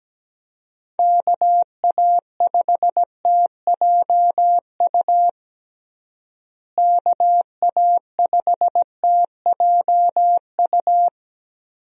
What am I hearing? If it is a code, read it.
Morse code, heard as a keyed tone, about 17 wpm, 700 Hz. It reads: KA5TJU KA5TJU
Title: USA Call Signs - 6 Characters - Repeated 2x - 17wpm